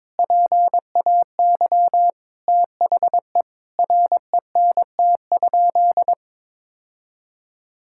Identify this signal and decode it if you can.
Morse code, heard as a keyed tone, about 22 wpm, 700 Hz. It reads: PAY THE RENT?
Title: Sentences from Top 1000 Words - Review Part 5 - 22wpm